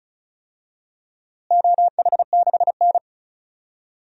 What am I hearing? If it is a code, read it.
Morse code, heard as a keyed tone, about 34 wpm, 700 Hz. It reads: OH6N
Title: DXCC-100 Call Signs - 4 Characters - Repeated 3x - 35wpm